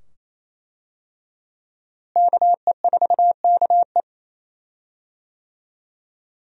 Morse code, heard as a keyed tone, about 28 wpm, 700 Hz. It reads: KE4KE
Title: USA Call Signs - 5 Characters - Repeated 2x - 28wpm